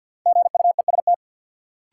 Morse code, heard as a keyed tone, about 52 wpm, 700 Hz. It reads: GUEST
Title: Sentences from Top 1600 Words - New Word Drill - 50wpm